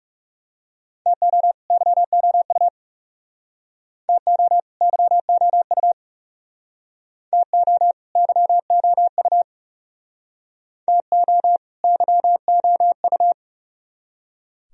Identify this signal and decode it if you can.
Morse code, heard as a keyed tone, about 39 wpm, 700 Hz. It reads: TO YOU TO YOU TO YOU TO YOU